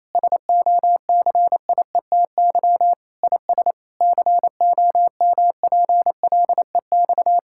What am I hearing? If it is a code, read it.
Morse code, heard as a keyed tone, about 28 wpm, 700 Hz. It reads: SOCIETY IS COMPLEX
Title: Sentences from Top 1000 Words - Review Part 6 - 28wpm